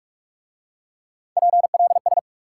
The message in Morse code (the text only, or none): PLS